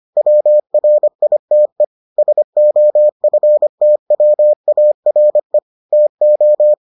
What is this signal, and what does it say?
Morse code, heard as a keyed tone, about 25 wpm, 600 Hz. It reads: WRITE SOFTWARE TO